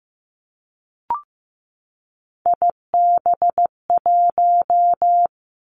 Morse code, heard as a keyed tone, about 15 wpm, 700 Hz. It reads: IB1
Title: Three Challenging Letter-Number Combinations 15wpm